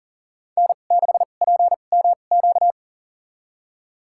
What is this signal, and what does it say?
Morse code, heard as a keyed tone, about 40 wpm, spaced as for 29 wpm, 700 Hz. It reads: N6PMQ
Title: USA Call Signs - 5 Characters - 2x Character Spacing - 40wpm